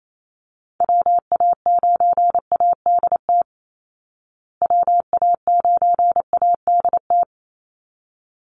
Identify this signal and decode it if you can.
Morse code, heard as a keyed tone, about 28 wpm, 700 Hz. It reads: WA9ADT WA9ADT
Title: Call Signs Repeated 3x - 28wpm